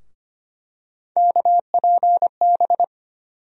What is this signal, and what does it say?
Morse code, heard as a keyed tone, about 25 wpm, 700 Hz. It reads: KPB
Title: USA Call Sign Postfixes - 3 Letters 25wpm